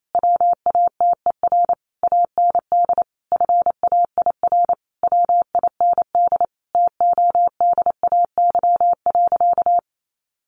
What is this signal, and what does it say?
Morse code, heard as a keyed tone, about 28 wpm, 700 Hz. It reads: WATER AND FAIR WIND TODAY.
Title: Sentences from Top 1400 Words 28wpm